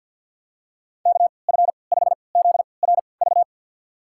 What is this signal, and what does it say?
Morse code, heard as a keyed tone, about 51 wpm, spaced as for 28 wpm, 700 Hz. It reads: KF5ZRV
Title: USA Call Signs - 3x Character Spacing - 50wpm